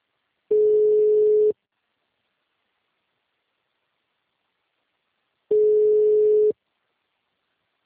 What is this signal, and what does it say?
Ringback tone (Europe, 425 Hz).